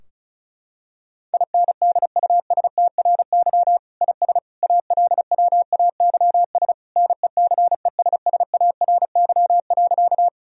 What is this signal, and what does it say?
Morse code, heard as a keyed tone, about 35 wpm, 700 Hz. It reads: INDUSTRY IS ALWAYS NECESSARY.
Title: Sentences from Top 1000 Words - Review Part 4 - 35wpm